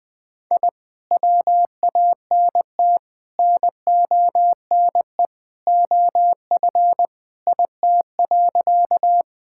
Morse code, heard as a keyed tone, about 21 wpm, 700 Hz. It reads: I WANT NONE OF IT.
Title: Sentences with 2-Word Phrases - Spoken Beginning - 20wpm